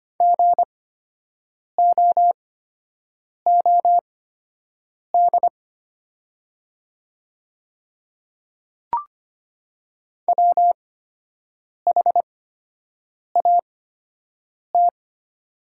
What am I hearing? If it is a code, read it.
Morse code, heard as a keyed tone, about 25 wpm, spaced as for 7 wpm, 700 Hz. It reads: GOOD WHAT
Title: 8x Character Spacing - Top 200 Words - 25wpm